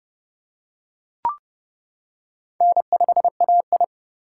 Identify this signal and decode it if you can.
Morse code, heard as a keyed tone, about 28 wpm, 700 Hz. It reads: N5AI